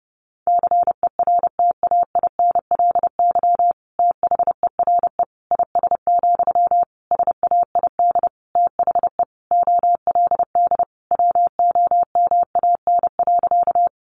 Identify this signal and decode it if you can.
Morse code, heard as a keyed tone, about 30 wpm, 700 Hz. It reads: CERTAINLY THERE IS, SAID THE OLD WOMAN.
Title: Sentences from Top 1100 Words 30wpm